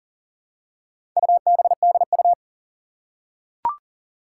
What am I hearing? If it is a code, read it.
Morse code, heard as a keyed tone, about 40 wpm, 690 Hz. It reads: UBDU
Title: Four Challenging Letter-Number Combinations 40wpm